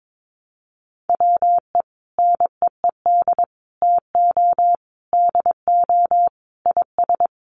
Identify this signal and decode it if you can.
Morse code, heard as a keyed tone, about 22 wpm, 700 Hz. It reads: WE NEED TO DO IS
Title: Sets of 5 Words Encore2 22wpm